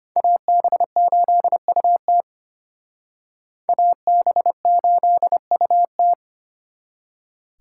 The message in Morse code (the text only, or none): AB8UT AB8UT